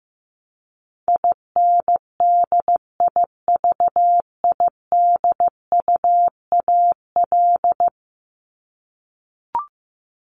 Morse code, heard as a keyed tone, about 15 wpm, 700 Hz. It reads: INDIVIDUAL
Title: ES:S Suffix - Top 100 Words with Base Words 15wpm